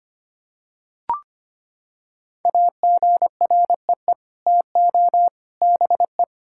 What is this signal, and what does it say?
Morse code, heard as a keyed tone, about 25 wpm, 700 Hz. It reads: AGREE TO BE
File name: Sets of 9 Words - 25wpm